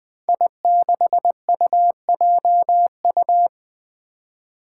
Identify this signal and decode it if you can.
Morse code, heard as a keyed tone, about 20 wpm, 700 Hz. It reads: I6UJU
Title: Five Challenging Letter-Number Combinations 20wpm